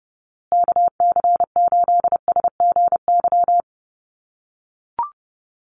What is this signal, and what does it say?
Morse code, heard as a keyed tone, about 28 wpm, 700 Hz. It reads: KC8SGY
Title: USA Call Signs Repeated 2x 30wpm